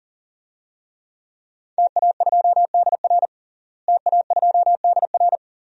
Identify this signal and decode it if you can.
Morse code, heard as a keyed tone, about 40 wpm, 700 Hz. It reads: TA2DR TA2DR
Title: DXCC-100 Call Signs - 5 Characters - Repeated 2x - 40wpm